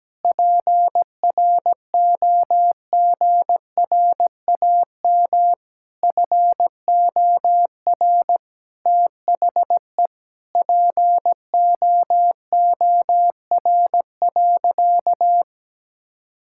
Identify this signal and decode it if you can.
Morse code, heard as a keyed tone, about 17 wpm, 700 Hz. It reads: PROGRAM FOR THE POOR.
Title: Sentences from Top 1000 Words - Review Part 5 - 17wpm